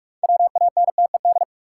Morse code, heard as a keyed tone, about 42 wpm, 700 Hz. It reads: WANTED
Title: Sets of 2 Words - New Word Drill - 45wpm